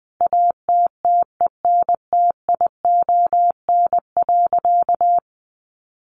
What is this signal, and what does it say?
Morse code, heard as a keyed tone, about 20 wpm, 700 Hz. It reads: ATTENTION.
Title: Sentences from Top 1300 Words 20wpm